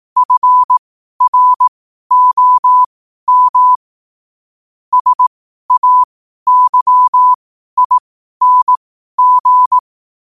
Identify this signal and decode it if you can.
Morse code, heard as a keyed tone, about 18 wpm, spaced as for 12 wpm, 1000 Hz. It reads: FROM SAYING
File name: qotdfiles / 12wpm